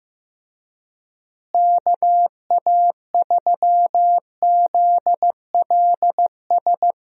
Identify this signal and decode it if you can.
Morse code, heard as a keyed tone, about 15 wpm, 700 Hz. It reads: KA3ZLS
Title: USA Call Signs - 6 Characters - Repeated 3x - 15wpm